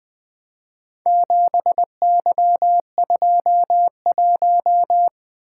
Morse code, heard as a keyed tone, about 20 wpm, 700 Hz. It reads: 7Y21
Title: Four Letter-Number Combinations - 20wpm